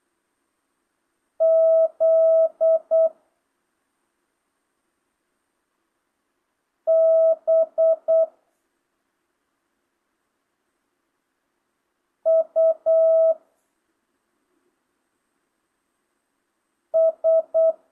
Morse code, heard as a keyed tone, about 8 wpm, 650 Hz. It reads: Z B U S
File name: jh8cbh / 2025cw